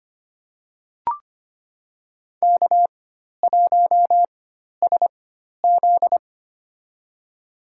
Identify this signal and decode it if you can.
Morse code, heard as a keyed tone, about 25 wpm, spaced as for 12 wpm, 700 Hz. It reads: K1SZ
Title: USA Call Signs - 4 Characters - 4x Character Spacing - 25wpm